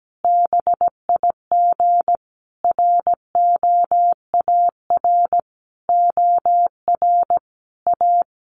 Morse code, heard as a keyed tone, about 17 wpm, 700 Hz. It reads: BIG ROAR OR A
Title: A Sound Of Thunder 17wpm